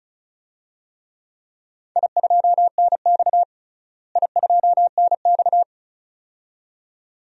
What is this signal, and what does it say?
Morse code, heard as a keyed tone, about 36 wpm, 700 Hz. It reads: I2NX I2NX
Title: DXCC-100 Call Signs - 4 Characters - Repeated 2x - 35wpm